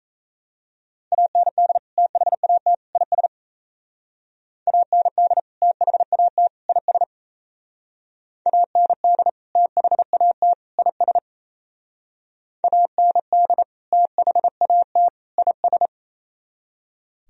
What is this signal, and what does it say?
Morse code, heard as a keyed tone, about 33 wpm, 700 Hz. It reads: AND THAT IS AND THAT IS AND THAT IS AND THAT IS
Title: Speed-Racing - Sets of 3 Words 28wpm